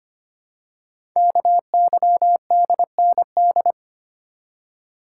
Morse code, heard as a keyed tone, about 25 wpm, 700 Hz. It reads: KYDND
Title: Code Groups - 5 Letters - 25wpm